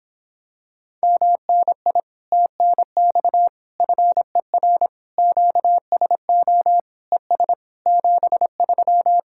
Morse code, heard as a keyed tone, about 26 wpm, 700 Hz. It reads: MNI TNX FER QSO ES 73